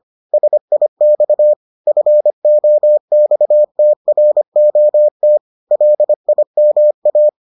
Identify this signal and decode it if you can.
Morse code, heard as a keyed tone, about 25 wpm, 600 Hz. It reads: SIX FOXTROT LIMA